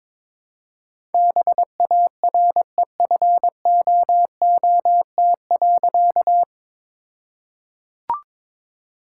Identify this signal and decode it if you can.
Morse code, heard as a keyed tone, about 22 wpm, 700 Hz. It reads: BAREFOOT.
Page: morse-code-ninja-practice-22wpm